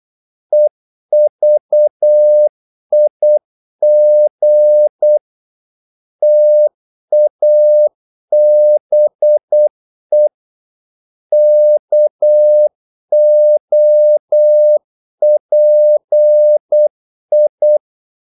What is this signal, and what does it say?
Morse code, heard as a keyed tone, about 8 wpm, 600 Hz. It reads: EVIG TABE KOPI
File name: Korte ord DK 8wpm